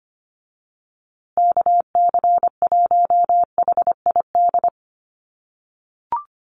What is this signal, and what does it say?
Morse code, heard as a keyed tone, about 25 wpm, 700 Hz. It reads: KC1HID
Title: USA Call Signs Repeated 3x 25wpm